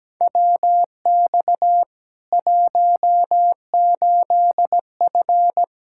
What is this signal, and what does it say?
Morse code, heard as a keyed tone, about 17 wpm, 700 Hz. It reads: WX 18F
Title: Practice QSO - Exchange 2 - Info - 17wpm